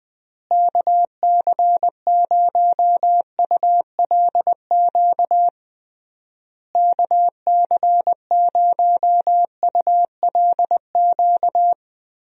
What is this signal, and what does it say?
Morse code, heard as a keyed tone, about 21 wpm, 700 Hz. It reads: KC0ULQ KC0ULQ
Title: USA Call Signs - 6 Characters - Repeated 3x - 20wpm